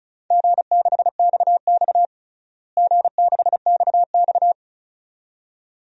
Morse code, heard as a keyed tone, about 35 wpm, 700 Hz. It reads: G6XX G6XX
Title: DXCC-100 Call Signs - 4 Characters - Repeated 2x - 35wpm